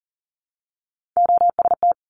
Morse code, heard as a keyed tone, about 41 wpm, 700 Hz. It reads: OST
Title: Top Three Letter Combinations in Words 40wpm